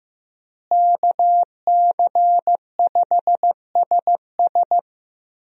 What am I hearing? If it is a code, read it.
Morse code, heard as a keyed tone, about 15 wpm, 700 Hz. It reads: KC5SS
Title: USA Call Signs - 5 Characters - Repeated 2x - 15wpm